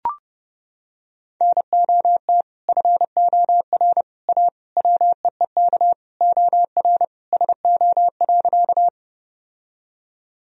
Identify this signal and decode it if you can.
Morse code, heard as a keyed tone, about 30 wpm, 700 Hz. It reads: NOT FOR A WEEK OR SO.
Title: Sentences from Top 1000 Words - Review Part 2 - 30wpm